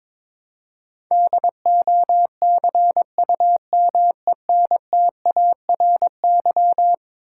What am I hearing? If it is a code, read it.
Morse code, heard as a keyed tone, about 22 wpm, 700 Hz. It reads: DOCUMENTARY
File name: practice / 22wpm / Eleven Letter Words - 22wpm